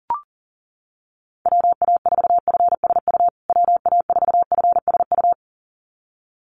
Morse code, heard as a keyed tone, about 40 wpm, 700 Hz. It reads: WA4FSU WA4FSU
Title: USA Call Signs Repeated 2x 40wpm